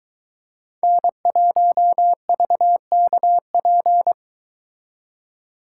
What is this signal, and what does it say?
Morse code, heard as a keyed tone, about 23 wpm, 700 Hz. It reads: N1VKP